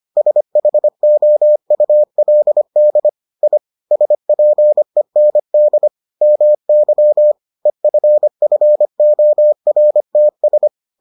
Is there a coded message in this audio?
Yes: Morse code from 0.2 to 10.7 s, heard as a keyed tone, about 25 wpm, 600 Hz. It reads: SHOULD I SPEND MY EFFORTS